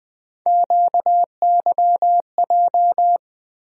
Morse code, heard as a keyed tone, about 20 wpm, 700 Hz. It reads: QYJ
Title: USA Call Sign Postfixes - 3 Letters 20wpm